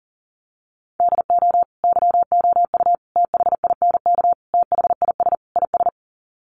Morse code, heard as a keyed tone, about 40 wpm, 700 Hz. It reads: DO YOU THINK THIS IS
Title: Sets of 5 Words Encore2 40wpm